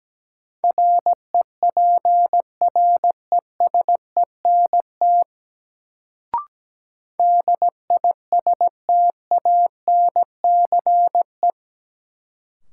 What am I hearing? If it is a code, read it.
Morse code, heard as a keyed tone, about 17 wpm, 700 Hz. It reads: REPRESENT DISTANCE
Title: Sentences from Top 1200 Words - New Word Drill Alt - 17wpm